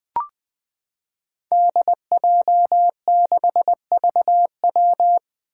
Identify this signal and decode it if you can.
Morse code, heard as a keyed tone, about 20 wpm, 700 Hz. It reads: DJ6VW
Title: Five Challenging Letter-Number Combinations 20wpm